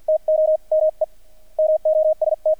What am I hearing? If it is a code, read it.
Morse code, heard as a keyed tone, about 48 wpm, spaced as for 41 wpm, 620 Hz, with radio noise. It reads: TO ME MOST
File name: archived / CRSnet